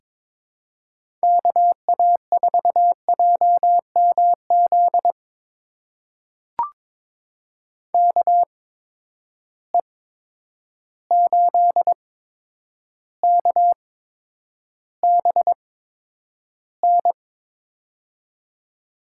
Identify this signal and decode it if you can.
Morse code, heard as a keyed tone, about 22 wpm, spaced as for 9 wpm, 700 Hz. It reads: KA4JMZ KE8KBN
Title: USA Call Signs - 6 Characters - 8x Character Spacing - 22wpm